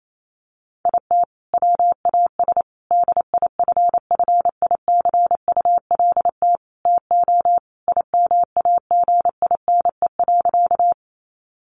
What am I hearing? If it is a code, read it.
Morse code, heard as a keyed tone, about 28 wpm, 700 Hz. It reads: IT WAS DIFFICULT TO IMAGINE.
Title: Sentences from Top 1400 Words 28wpm